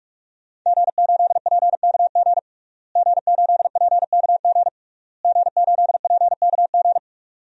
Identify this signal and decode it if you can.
Morse code, heard as a keyed tone, about 42 wpm, 700 Hz. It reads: G8PKG G8PKG G8PKG
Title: DXCC-100 Call Signs - 5 Characters - Repeated 3x - 45wpm